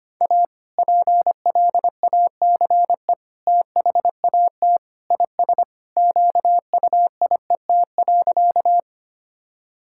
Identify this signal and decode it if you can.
Morse code, heard as a keyed tone, about 25 wpm, 690 Hz. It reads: A PLACE THAT IS QUIET.
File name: practice / 25wpm / Sentences with 2-Word Phrases - 25wpm